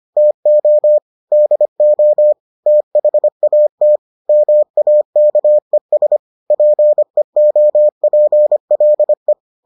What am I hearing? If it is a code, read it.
Morse code, heard as a keyed tone, about 25 wpm, 600 Hz. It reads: TO DO THAT MAKES PEOPLE